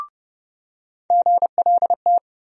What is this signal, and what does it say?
Morse code, heard as a keyed tone, about 29 wpm, 700 Hz. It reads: GLT
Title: USA Call Sign Postfixes - 3 Letters 30wpm